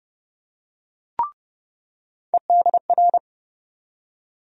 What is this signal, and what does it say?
Morse code, heard as a keyed tone, about 30 wpm, 700 Hz. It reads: EDR
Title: USA Call Sign Postfixes - 3 Letters 30wpm